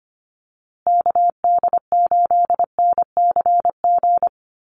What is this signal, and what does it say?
Morse code, heard as a keyed tone, about 25 wpm, 700 Hz. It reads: KD8NCG
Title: USA Call Signs Repeated 3x 25wpm